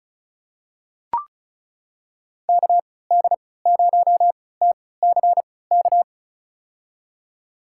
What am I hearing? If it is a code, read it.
Morse code, heard as a keyed tone, about 36 wpm, spaced as for 20 wpm, 700 Hz. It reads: KD0TCK
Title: USA Call Signs - 6 Characters - 3x Character Spacing - 35wpm